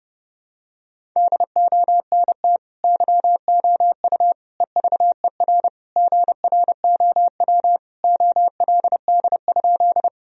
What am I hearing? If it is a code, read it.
Morse code, heard as a keyed tone, about 30 wpm, 700 Hz. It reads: DONT YOU EVER GROW OLD?
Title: Sentences from Top 1000 Words - Review Part 1 - 30wpm